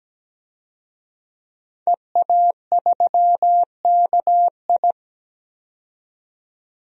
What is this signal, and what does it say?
Morse code, heard as a keyed tone, about 17 wpm, 700 Hz. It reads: EA3KI
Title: DXCC-100 Call Signs - 5 Characters - 17wpm